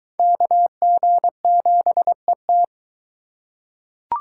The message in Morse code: KG7ET

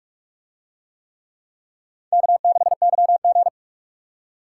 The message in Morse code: K6YG

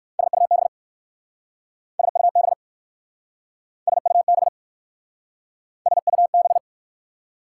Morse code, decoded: SUB SUB SUB SUB

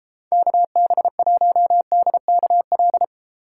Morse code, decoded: KB1DKL

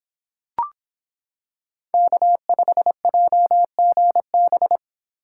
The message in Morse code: K5JGB

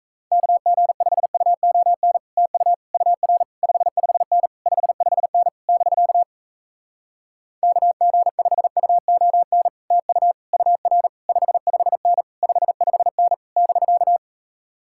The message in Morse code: KG5UON TU UR 55N 55N <BK> KG5UON TU UR 55N 55N <BK>

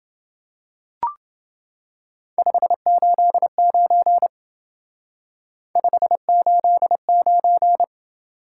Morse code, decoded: 589 589

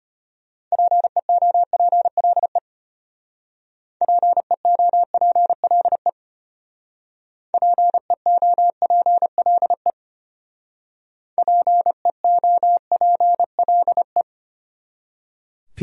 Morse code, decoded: PEOPLE PEOPLE PEOPLE PEOPLE

